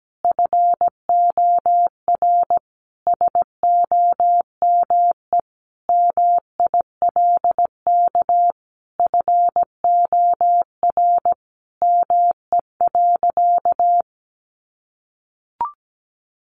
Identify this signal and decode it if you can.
Morse code, heard as a keyed tone, about 17 wpm, 700 Hz. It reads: FOR SOME MILK FOR ME.
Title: Sentences from Top 1500 Words 17wpm